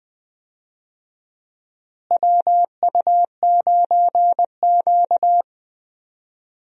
Morse code, heard as a keyed tone, about 20 wpm, 700 Hz. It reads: WU9Q
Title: USA Call Signs - 4 Characters - Repeated 2x - 20wpm